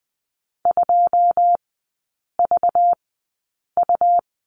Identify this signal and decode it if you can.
Morse code, heard as a keyed tone, about 20 wpm, spaced as for 8 wpm, 690 Hz. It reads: 2VU
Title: 198 Lesson 32e - Letter k - Three Characters 20wpm